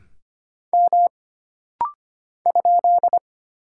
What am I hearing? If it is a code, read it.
Morse code, heard as a keyed tone, about 25 wpm, 700 Hz. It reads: M ?
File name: Single Character (Letter:Number:Punctuation:Prosign) - Rapid-Fire - 25wpm